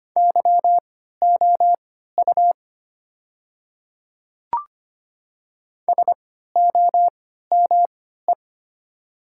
Morse code, heard as a keyed tone, about 25 wpm, spaced as for 14 wpm, 700 Hz. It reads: YOU SOME
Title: 3x Character Spacing - Top 100 Words - 25wpm